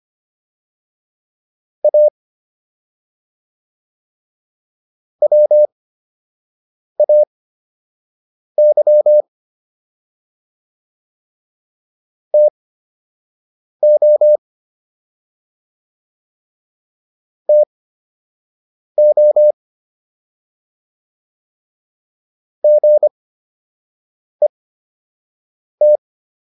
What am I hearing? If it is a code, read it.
Morse code, heard as a keyed tone, about 25 wpm, spaced as for 6 wpm, 600 Hz. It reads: A WAY TO TO GET